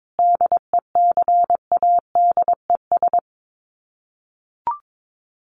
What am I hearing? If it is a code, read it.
Morse code, heard as a keyed tone, about 22 wpm, 700 Hz. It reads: DECADES
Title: Sets of 5 Words Encore2 22wpm